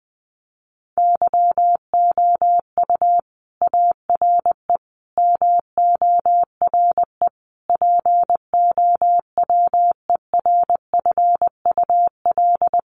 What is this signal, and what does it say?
Morse code, heard as a keyed tone, about 20 wpm, 700 Hz. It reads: YOU ARE MORE POWERFUL
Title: Sentences from Top 1300 Words 20wpm